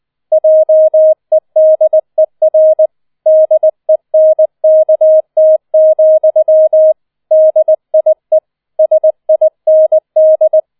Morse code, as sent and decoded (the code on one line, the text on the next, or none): .--- . -.. .-. / -.. -. -.- - --..-- / -.. .. / ... .. -. -..
JEDER DENKT, DIE SIND